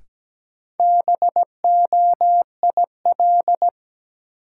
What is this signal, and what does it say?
Morse code, heard as a keyed tone, about 17 wpm, 700 Hz. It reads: BOIL
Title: Sets of 5 Words Encore - New Word Drill - 17wpm